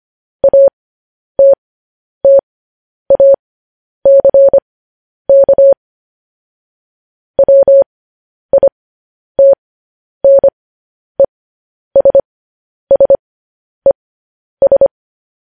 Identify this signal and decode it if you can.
Morse code, heard as a keyed tone, about 25 wpm, spaced as for 10 wpm, 550 Hz. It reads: ATTACK WITNESSES
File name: podcast / News headlines-1x-25x10